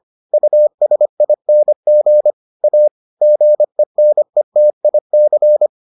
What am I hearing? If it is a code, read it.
Morse code, heard as a keyed tone, about 25 wpm, 600 Hz. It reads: USING A GENETIC